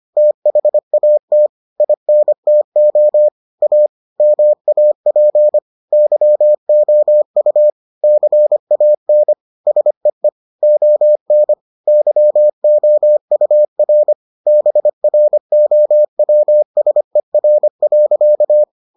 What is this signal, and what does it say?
Morse code, heard as a keyed tone, about 25 wpm, 600 Hz. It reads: THAT INTO A MAP YOU CAN SEE ON YOUR BROWSER.